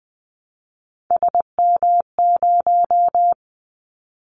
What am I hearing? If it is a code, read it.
Morse code, heard as a keyed tone, about 20 wpm, 700 Hz. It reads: SM0
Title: Callsign Prefixes for Top 25 DXCC - 20wpm